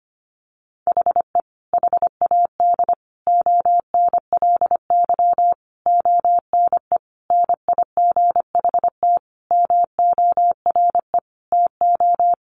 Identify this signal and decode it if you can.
Morse code, heard as a keyed tone, about 25 wpm, 700 Hz. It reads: HE HAD ONLY ONE NIGHT MORE TO
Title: Sentences from Top 300 Words 25wpm